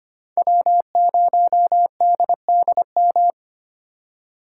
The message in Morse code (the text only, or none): W0DDM